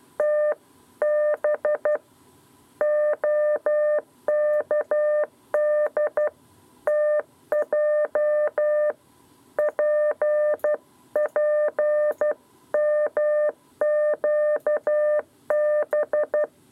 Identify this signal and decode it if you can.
Morse code, heard as a keyed tone, about 12 wpm, 600 Hz. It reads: TB OKDTJ PPMQB
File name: 7b lezione cw 2